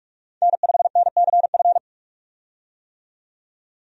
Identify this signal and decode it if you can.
Morse code, heard as a keyed tone, about 43 wpm, 700 Hz. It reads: N5NCF
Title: USA Call Signs - 5 Characters - 45wpm